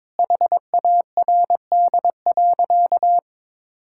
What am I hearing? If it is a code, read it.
Morse code, heard as a keyed tone, about 22 wpm, 700 Hz. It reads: HARD.
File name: practice / 22wpm / Sentences with 4-Word Phrases - 22wpm